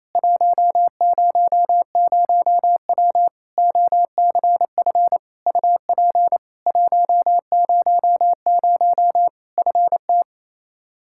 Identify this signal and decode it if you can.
Morse code, heard as a keyed tone, about 28 wpm, 700 Hz. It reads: 100W OCF UP 100 FT
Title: Practice QSO - Exchange 2 - Info - 28wpm